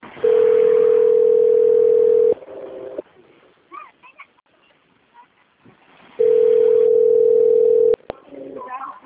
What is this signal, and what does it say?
Ringback tone (North America), 440 + 480 Hz.